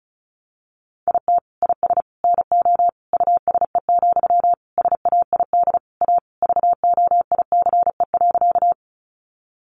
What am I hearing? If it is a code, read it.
Morse code, heard as a keyed tone, about 34 wpm, 700 Hz. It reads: IT IS NO USE, SAID A VOICE.
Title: Sentences from Top 500 Words 35wpm